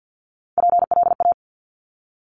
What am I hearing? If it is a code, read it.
Morse code, heard as a keyed tone, about 50 wpm, 700 Hz. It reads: PLA
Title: Top Three Letter Combinations in Words 50wpm